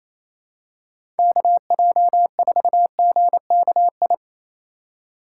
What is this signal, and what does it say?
Morse code, heard as a keyed tone, about 28 wpm, 700 Hz. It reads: KJ4GKI
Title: USA Call Signs - 6 Characters - 4x Character Spacing - 28wpm